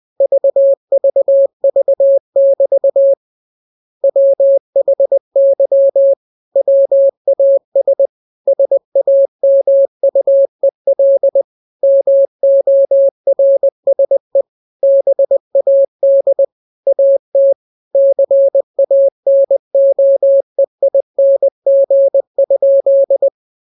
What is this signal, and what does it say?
Morse code, heard as a keyed tone, about 20 wpm, 550 Hz. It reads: VVV<BT> WHY WAS SAMUEL MORSE BAD AT CANOEING?
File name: jokes / joke 1 20 0000